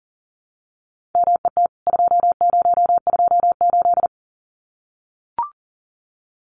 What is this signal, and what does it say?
Morse code, heard as a keyed tone, about 40 wpm, 700 Hz. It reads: MET 2028
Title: CWT Exchange 40wpm